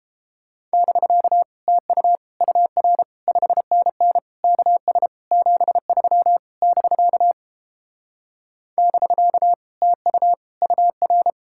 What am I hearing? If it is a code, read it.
Morse code, heard as a keyed tone, about 31 wpm, 700 Hz. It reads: <BK> TU UR 5NN KS 73 <BK> <BK> TU UR